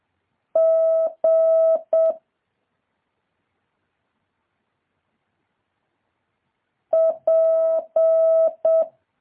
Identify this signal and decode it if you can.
Morse code, heard as a keyed tone, about 7 wpm, 650 Hz. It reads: G P
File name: jh8cbh / 2023cw